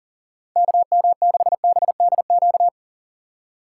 Morse code, heard as a keyed tone, about 40 wpm, 700 Hz. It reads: KM6BDQ
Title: USA Call Signs - 6 Characters - 6x Character Spacing - 40wpm